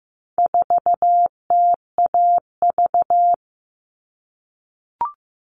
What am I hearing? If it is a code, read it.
Morse code, heard as a keyed tone, about 15 wpm, 700 Hz. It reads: 4TAV
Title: All Cut Numbers - 4-Digit 15wpm